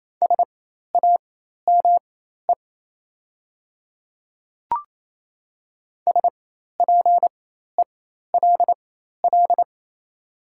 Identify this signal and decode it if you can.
Morse code, heard as a keyed tone, about 28 wpm, spaced as for 13 wpm, 690 Hz. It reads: SAME SPELL
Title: 4x Character Spacing - Top 300 Words - 28wpm